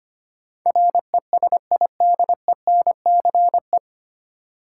Morse code, heard as a keyed tone, about 25 wpm, 700 Hz. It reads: RESIDENCE
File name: Nine Letter Words - 25wpm